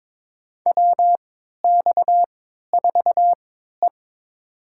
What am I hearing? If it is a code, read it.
Morse code, heard as a keyed tone, about 22 wpm, spaced as for 13 wpm, 700 Hz. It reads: WX4E
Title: USA Call Signs - 4 Characters - 3x Character Spacing - 22wpm